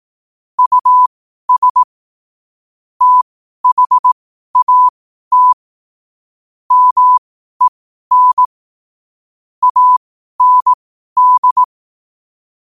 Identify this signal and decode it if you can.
Morse code, heard as a keyed tone, about 18 wpm, spaced as for 12 wpm, 1000 Hz. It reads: US THAT MEN AND
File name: qotdfiles / 12wpm